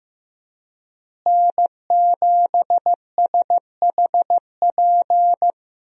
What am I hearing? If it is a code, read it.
Morse code, heard as a keyed tone, about 16 wpm, 700 Hz. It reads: N7SHP
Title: USA Call Signs - 5 Characters - Repeated 2x - 15wpm